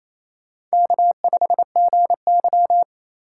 Morse code, heard as a keyed tone, about 34 wpm, 700 Hz. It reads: K5GY
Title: Speed Racing - USA 1x2 Call Signs 28wpm